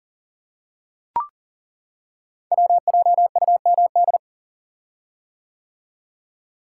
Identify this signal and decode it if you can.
Morse code, heard as a keyed tone, about 37 wpm, 690 Hz. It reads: WJUMD
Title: Code Groups - 5 Letters - 40wpm